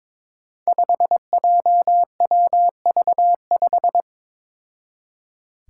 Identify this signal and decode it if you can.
Morse code, heard as a keyed tone, about 21 wpm, 700 Hz. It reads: 5JWV5
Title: Five Challenging Letter-Number Combinations 22wpm